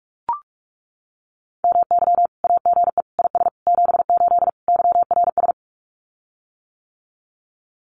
Morse code, heard as a keyed tone, about 44 wpm, 700 Hz. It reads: MY AGE IS 78 YRS
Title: Common QSO Phrases Encore 45wpm